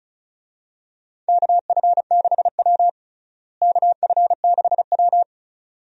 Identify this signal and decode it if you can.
Morse code, heard as a keyed tone, about 36 wpm, 700 Hz. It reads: KF6W KF6W
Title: USA Call Signs - 4 Characters - Repeated 2x - 35wpm